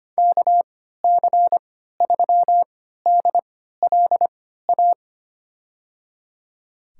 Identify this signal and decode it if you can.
Morse code, heard as a keyed tone, about 25 wpm, spaced as for 14 wpm, 700 Hz. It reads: KC3DLA
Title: USA Call Signs - 3x Character Spacing - 25wpm